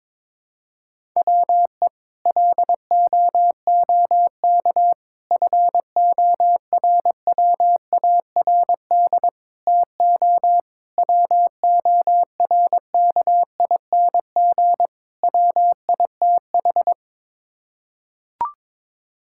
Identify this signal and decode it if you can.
Morse code, heard as a keyed tone, about 22 wpm, 700 Hz. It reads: WE LOOK FORWARD TO WORKING WITH
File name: practice / 22wpm / Sets of 6 Words - 22wpm